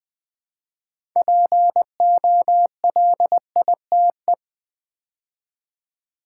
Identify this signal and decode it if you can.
Morse code, heard as a keyed tone, about 20 wpm, 690 Hz. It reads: POLITE
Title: Build A Word - 2 Letters At A Time - 6-Letter Words - 20wpm